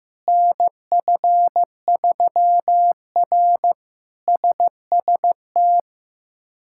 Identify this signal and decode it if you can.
Morse code, heard as a keyed tone, about 15 wpm, 700 Hz. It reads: NF3R SST
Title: K1USN SST QSO Practice - 15wpm